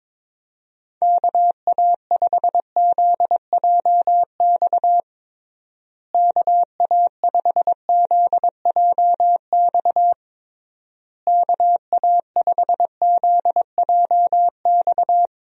Morse code, heard as a keyed tone, about 22 wpm, 700 Hz. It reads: KA5ZJX KA5ZJX KA5ZJX
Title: USA Call Signs - 6 Characters - Repeated 3x - 22wpm